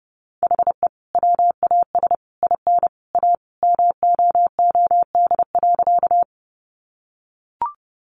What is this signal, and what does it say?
Morse code, heard as a keyed tone, about 30 wpm, 700 Hz. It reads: HE WAS IN A MOOD.
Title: Sentences from Top 1100 Words 30wpm